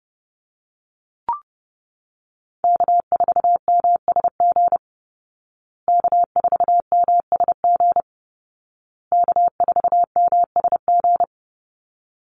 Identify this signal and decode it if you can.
Morse code, heard as a keyed tone, about 30 wpm, 700 Hz. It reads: K4MSG K4MSG K4MSG
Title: Call Signs Repeated 2x - 30wpm